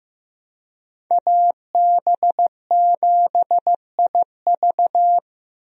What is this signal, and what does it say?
Morse code, heard as a keyed tone, about 15 wpm, 700 Hz. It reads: AB7IV
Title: USA Call Signs - 5 Characters - Repeated 3x - 15wpm